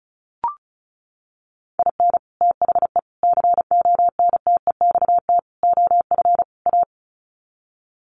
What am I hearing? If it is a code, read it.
Morse code, heard as a keyed tone, about 34 wpm, 700 Hz. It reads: IN THE CONTEXT OF A
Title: Sets of 5 Words Encore 35wpm